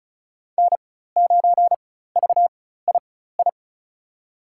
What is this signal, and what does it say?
Morse code, heard as a keyed tone, about 34 wpm, spaced as for 16 wpm, 700 Hz. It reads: N9VII
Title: USA Call Signs - 5 Characters - 4x Character Spacing - 35wpm